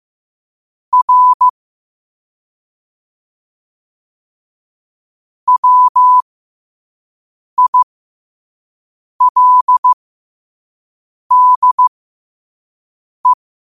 Morse code, heard as a keyed tone, about 15 wpm, spaced as for 5 wpm, 1000 Hz. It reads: R WILDE
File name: 5wpm